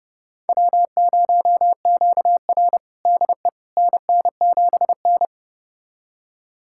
Morse code, heard as a keyed tone, about 32 wpm, 700 Hz. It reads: W0QR DE NN7N
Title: USA Callsign DE Callsign - 4 Characters - 30wpm